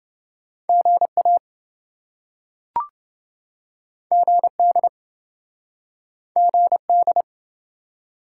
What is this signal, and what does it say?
Morse code, heard as a keyed tone, about 28 wpm, 700 Hz. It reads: GA GD GD